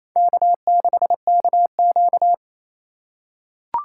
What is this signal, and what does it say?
Morse code, heard as a keyed tone, about 28 wpm, 700 Hz. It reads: K6KQ
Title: USA Call Signs - 4 Characters - 4x Character Spacing - 28wpm